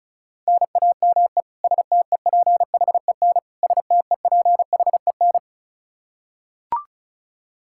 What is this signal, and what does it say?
Morse code, heard as a keyed tone, about 36 wpm, 700 Hz. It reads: NAME STEPHEN STEPHEN
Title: Practice QSO - Exchange 1 - Beginning - 35wpm